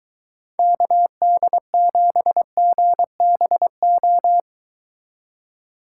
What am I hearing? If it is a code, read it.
Morse code, heard as a keyed tone, about 23 wpm, 700 Hz. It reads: KD7GBO